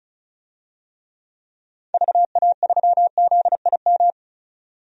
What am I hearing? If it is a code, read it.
Morse code, heard as a keyed tone, about 34 wpm, 700 Hz. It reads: VA3ZIM
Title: DXCC-100 Call Signs - 6 Characters - 35wpm